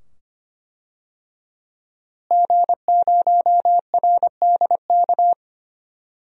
Morse code, heard as a keyed tone, about 25 wpm, 700 Hz. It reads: G0RDK
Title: DXCC-100 Call Signs - 5 Characters - Repeated 2x - 25wpm